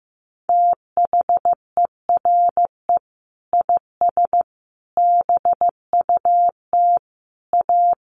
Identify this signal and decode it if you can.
Morse code, heard as a keyed tone, about 15 wpm, 700 Hz. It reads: THERE IS BUT A
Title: Sentences from Top 1400 Words 15wpm